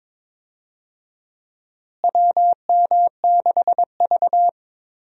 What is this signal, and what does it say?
Morse code, heard as a keyed tone, about 22 wpm, 700 Hz. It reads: WM6V
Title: USA Call Signs - 4 Characters - Repeated 2x - 22wpm